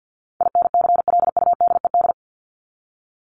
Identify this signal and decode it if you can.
Morse code, heard as a keyed tone, about 50 wpm, 700 Hz. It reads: INCLUDED
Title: ED Suffix - Top 100 Words with Base Words 50wpm